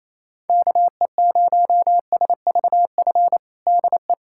Morse code, heard as a keyed tone, about 27 wpm, 700 Hz. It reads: KE0SVF DE
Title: USA Callsign DE Callsign - 6 Characters - 28wpm